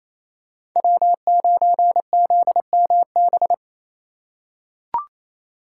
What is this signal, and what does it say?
Morse code, heard as a keyed tone, about 28 wpm, 700 Hz. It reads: W9ZMB
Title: USA Call Signs - 5 Characters - 4x Character Spacing - 28wpm